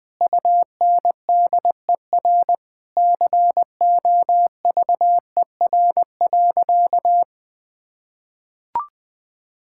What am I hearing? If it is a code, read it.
Morse code, heard as a keyed tone, about 20 wpm, 700 Hz. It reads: UNDER COVER.
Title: Sentences from Top 1000 Words - Review Part 6 - 20wpm